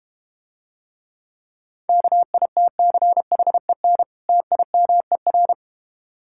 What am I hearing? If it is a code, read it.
Morse code, heard as a keyed tone, about 32 wpm, 700 Hz. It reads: KITCHEN TIMER